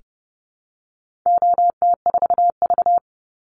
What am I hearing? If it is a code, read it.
Morse code, heard as a keyed tone, about 30 wpm, 700 Hz. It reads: OT4V
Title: Call Signs Repeated 2x - 30wpm